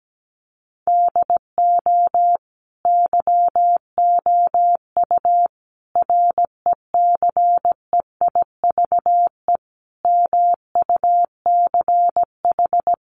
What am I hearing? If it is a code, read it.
Morse code, heard as a keyed tone, about 17 wpm, 700 Hz. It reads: DO YOU RECEIVE MUCH
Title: Sentences from Top 1000 Words 17wpm